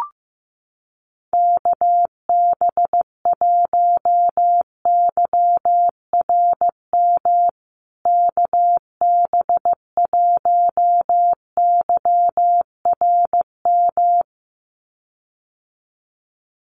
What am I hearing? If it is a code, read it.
Morse code, heard as a keyed tone, about 15 wpm, 700 Hz. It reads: KB1YRM KB1YRM
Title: USA Call Signs Repeated 2x 15wpm